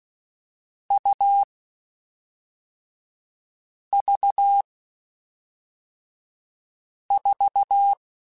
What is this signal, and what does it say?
Morse code, heard as a keyed tone, about 16 wpm, 800 Hz. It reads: U V 4